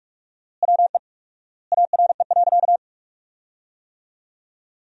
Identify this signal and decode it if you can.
Morse code, heard as a keyed tone, about 43 wpm, 700 Hz. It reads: WE ARE.
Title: 4x Word-Spacing - Sentences from Top 200 Words - 45wpm